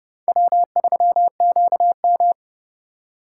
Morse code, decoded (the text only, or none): W3QM